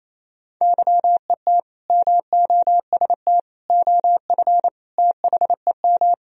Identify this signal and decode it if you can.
Morse code, heard as a keyed tone, about 28 wpm, 700 Hz. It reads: YET MOST OF THEM
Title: Sentences from Top 1000 Words - Review Part 1 - 28wpm